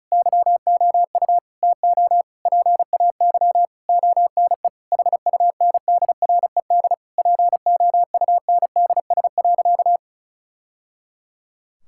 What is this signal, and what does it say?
Morse code, heard as a keyed tone, about 36 wpm, 700 Hz. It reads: YOU TO PAY ONE HUNDRED POUNDS.
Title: Sentences from Top 1000 Words - Review Part 6 - 35wpm